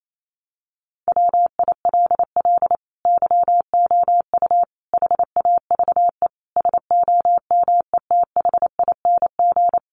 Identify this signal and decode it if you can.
Morse code, heard as a keyed tone, about 28 wpm, 700 Hz. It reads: WILL YOU HAVE SOMETHING
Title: Sentences from Top 600 Words 28wpm